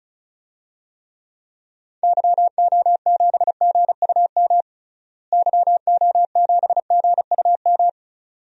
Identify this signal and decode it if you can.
Morse code, heard as a keyed tone, about 36 wpm, 700 Hz. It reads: YO7GUM YO7GUM
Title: DXCC-100 Call Signs - 6 Characters - Repeated 2x - 35wpm